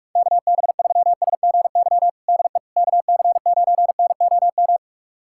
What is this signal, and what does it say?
Morse code, heard as a keyed tone, about 46 wpm, 700 Hz. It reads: KB3SGY DE KC9NOM
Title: USA Callsign DE Callsign - 6 Characters - 45wpm